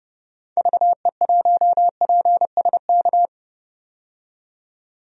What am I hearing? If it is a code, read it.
Morse code, heard as a keyed tone, about 30 wpm, 700 Hz. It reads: VE1PSK
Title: DXCC-100 Call Signs - 6 Characters - 30wpm